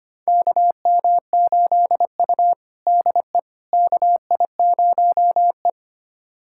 Morse code, heard as a keyed tone, about 25 wpm, 700 Hz. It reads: KM8U DE KI0E
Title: USA Callsign DE Callsign - 4 Characters - 25wpm